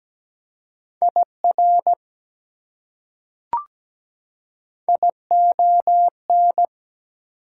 Morse code, heard as a keyed tone, about 17 wpm, 700 Hz. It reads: IR ION
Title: Prefixes and Suffixes 17wpm